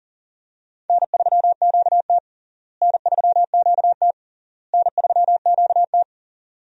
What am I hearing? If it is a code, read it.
Morse code, heard as a keyed tone, about 40 wpm, 700 Hz. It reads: N3QT N3QT N3QT
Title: USA Call Signs - 4 Characters - Repeated 3x - 40wpm